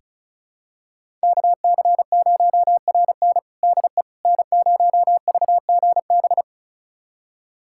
Morse code, heard as a keyed tone, about 34 wpm, 700 Hz. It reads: KC0RN DE N0VGB
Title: USA Callsign DE Callsign 35wpm